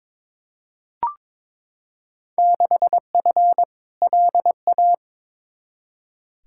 Morse code, heard as a keyed tone, about 21 wpm, 700 Hz. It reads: 6F LA
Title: Field Day Exchanges Encore - 22wpm